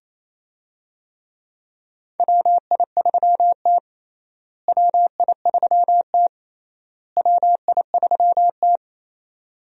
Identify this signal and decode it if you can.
Morse code, heard as a keyed tone, about 28 wpm, 700 Hz. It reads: WI3T WI3T WI3T
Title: USA Call Signs - 4 Characters - Repeated 3x - 28wpm